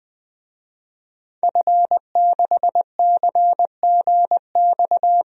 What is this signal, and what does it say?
Morse code, heard as a keyed tone, about 20 wpm, 700 Hz. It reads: F6CGX
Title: DXCC-100 Call Signs - 5 Characters - Repeated 3x - 20wpm